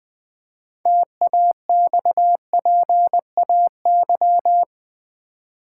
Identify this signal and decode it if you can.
Morse code, heard as a keyed tone, about 20 wpm, 700 Hz. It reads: TAXPAY
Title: Build A Word - 3 Letters At A Time - 8-Letter Words - 20wpm